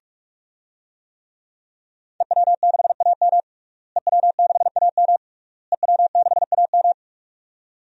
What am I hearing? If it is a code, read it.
Morse code, heard as a keyed tone, about 42 wpm, 700 Hz. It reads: EW6AM EW6AM EW6AM
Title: DXCC-100 Call Signs - 5 Characters - Repeated 3x - 45wpm